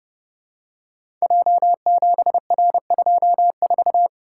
Morse code, heard as a keyed tone, about 30 wpm, 700 Hz. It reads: J7R24
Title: Code Groups - 5 Letter:Numbers - 30wpm